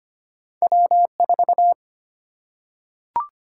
Morse code, heard as a keyed tone, about 25 wpm, 700 Hz. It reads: W 4
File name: Two Letter-Number Combinations - 25wpm